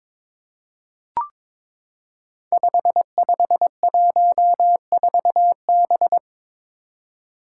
Morse code, heard as a keyed tone, about 23 wpm, 700 Hz. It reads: 5514B
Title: Five Challenging Letter-Number Combinations 22wpm